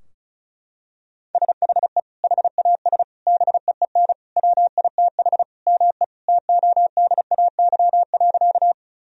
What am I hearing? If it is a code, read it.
Morse code, heard as a keyed tone, about 34 wpm, 700 Hz. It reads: SHE HAS BEEN WITH ME TODAY.
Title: Sentences from Top 1000 Words - Review Part 4 - 35wpm